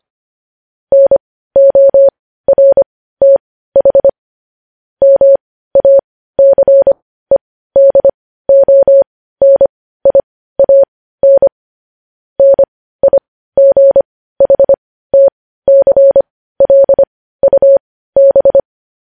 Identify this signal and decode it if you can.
Morse code, heard as a keyed tone, about 25 wpm, spaced as for 15 wpm, 550 Hz. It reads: NORTH MACEDONIAN NIGHTCLUB
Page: news-headlines-in-morse-code-at-15-wpm